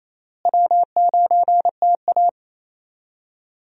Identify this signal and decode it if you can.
Morse code, heard as a keyed tone, about 28 wpm, 700 Hz. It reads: W9TA
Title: USA Call Signs - 4 Characters - Repeated 3x - 28wpm